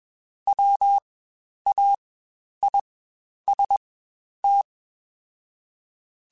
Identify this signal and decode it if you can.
Morse code, heard as a keyed tone, about 21 wpm, spaced as for 10 wpm, 800 Hz. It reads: WAIST
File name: Spaced / Chuckles Spaced-mp3s / MiddleAge sp @20